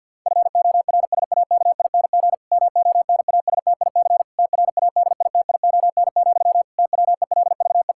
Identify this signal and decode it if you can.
Morse code, heard as a keyed tone, about 50 wpm, 700 Hz. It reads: FORSAKING MONASTIC TRADITION, TWELVE
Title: Pangram Challenge v2 - 50wpm